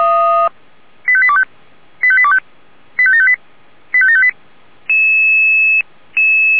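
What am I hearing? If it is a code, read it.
Five-tone selective call (ZVEI), heard as quick runs of five tones, tones of about 70 ms. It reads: 85825 85825 85758 85758
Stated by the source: ZVEI five-tone selective call 85735, received by SDR.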